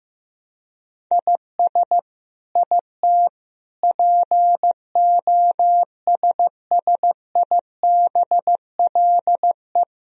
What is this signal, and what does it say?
Morse code, heard as a keyed tone, about 15 wpm, 700 Hz. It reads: IS IT POSSIBLE
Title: Sentences from Top 1000 Words - Review Part 3 - 15wpm